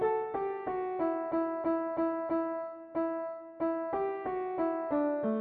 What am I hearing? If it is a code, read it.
There is no code.